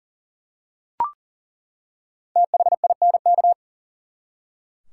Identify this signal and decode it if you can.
Morse code, heard as a keyed tone, about 40 wpm, 700 Hz. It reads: THINK
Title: Five Letter Words - 40wpm